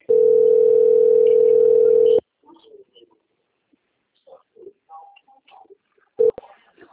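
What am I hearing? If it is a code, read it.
Ringback tone (North America), 440 + 480 Hz.
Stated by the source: Outgoing telephone call recorded on the TLMK - Rio de janeiro 2 BKP system.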